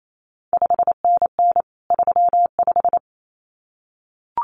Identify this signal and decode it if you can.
Morse code, heard as a keyed tone, about 28 wpm, 700 Hz. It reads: 5NN 35
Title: CQ WW DX Contest Exchange Encore - 28wpm